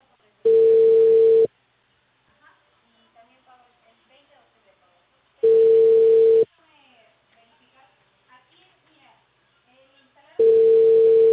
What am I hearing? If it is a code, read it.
Ringback tone (China, 450 Hz), with speech.